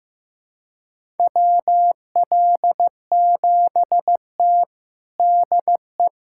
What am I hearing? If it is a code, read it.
Morse code, heard as a keyed tone, about 15 wpm, 700 Hz. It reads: WL7T DE
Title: USA Callsign DE Callsign - 4 Characters - 15wpm